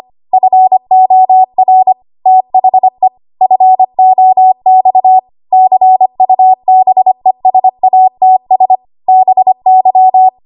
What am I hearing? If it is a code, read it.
Morse code, heard as a keyed tone, about 25 wpm, 750 Hz. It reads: FOR THE FOX CUBESATS BY
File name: Archive / 25 WPM / 25WPM